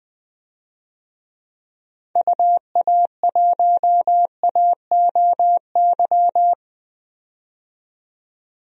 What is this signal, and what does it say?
Morse code, heard as a keyed tone, about 20 wpm, 700 Hz. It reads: UA1AOY
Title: DXCC-100 Call Signs - 6 Characters - 20wpm